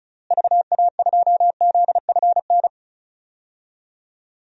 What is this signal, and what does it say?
Morse code, heard as a keyed tone, about 36 wpm, 700 Hz. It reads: VA2ZFN
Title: DXCC-100 Call Signs - 6 Characters - Repeated 3x - 35wpm